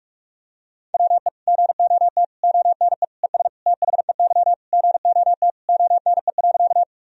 Morse code, heard as a keyed tone, about 42 wpm, 700 Hz. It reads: WE GOT ONE ES THEY GOT ONE.
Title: Sentences from Top 300 Words and Abbreviations 45wpm